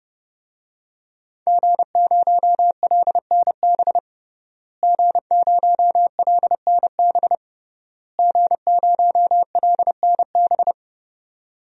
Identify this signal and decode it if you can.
Morse code, heard as a keyed tone, about 30 wpm, 700 Hz. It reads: G0LNB G0LNB G0LNB
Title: DXCC-100 Call Signs - 5 Characters - Repeated 3x - 30wpm